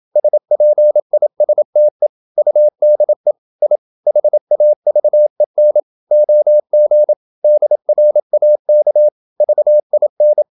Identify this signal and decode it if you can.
Morse code, heard as a keyed tone, about 27 wpm, 600 Hz. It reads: SPISTE UDE I HAVEN OG DRAK VIN
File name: Ferie QRQ 27wpm